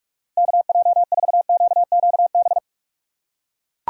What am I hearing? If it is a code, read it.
Morse code, heard as a keyed tone, about 42 wpm, 700 Hz. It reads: KJ4QQB
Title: USA Call Signs - 6 Characters - 8x Character Spacing - 45wpm